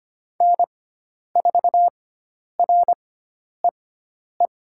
Morse code, heard as a keyed tone, about 25 wpm, spaced as for 10 wpm, 700 Hz. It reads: N4REE
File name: Lesson 16g - Letter c - Call Signs 25wpm